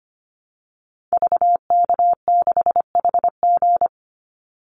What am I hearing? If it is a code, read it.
Morse code, heard as a keyed tone, about 25 wpm, 700 Hz. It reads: VK6HG
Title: Call Signs - 25wpm